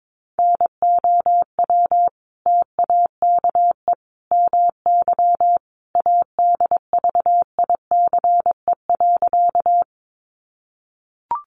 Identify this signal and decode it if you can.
Morse code, heard as a keyed tone, about 22 wpm, 700 Hz. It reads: NOW TAKE MY ADVICE.
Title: Sentences from Top 1100 Words 22wpm